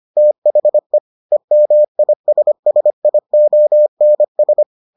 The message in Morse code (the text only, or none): THE EMISSIONS